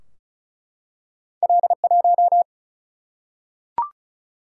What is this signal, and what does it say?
Morse code, heard as a keyed tone, about 36 wpm, 700 Hz. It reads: L1
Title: Single-Letter Single-Number 35wpm